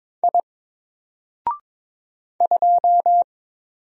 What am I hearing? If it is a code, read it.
Morse code, heard as a keyed tone, about 22 wpm, 700 Hz. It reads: I 2
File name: Single-Letter-Number-22wpm